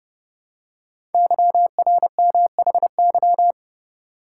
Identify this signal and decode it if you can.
Morse code, heard as a keyed tone, about 30 wpm, 700 Hz. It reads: YRMHY
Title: Code Groups - 5 Letters - 30wpm